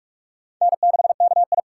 Morse code, heard as a keyed tone, about 41 wpm, 700 Hz. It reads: N6KI
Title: USA Call Signs - 4 Characters - 2x Character Spacing - 45wpm